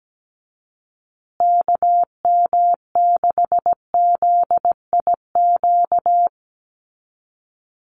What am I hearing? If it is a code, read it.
Morse code, heard as a keyed tone, about 17 wpm, 700 Hz. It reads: KM6ZIQ
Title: USA Call Signs Repeated 2x 17wpm